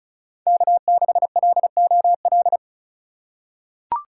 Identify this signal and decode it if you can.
Morse code, heard as a keyed tone, about 36 wpm, 700 Hz. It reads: K6LOL
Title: USA Call Signs - 5 Characters - 2x Character Spacing - 35wpm